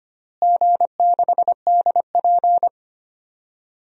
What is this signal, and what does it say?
Morse code, heard as a keyed tone, about 25 wpm, 700 Hz. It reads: G6DP
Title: DXCC-100 Call Signs - 4 Characters - Repeated 3x - 25wpm